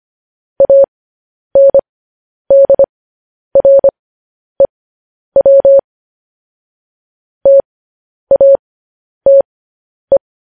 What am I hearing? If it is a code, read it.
Morse code, heard as a keyed tone, about 25 wpm, spaced as for 10 wpm, 550 Hz. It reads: ANDREW TATE
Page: news headlines in morse code at 10 wpm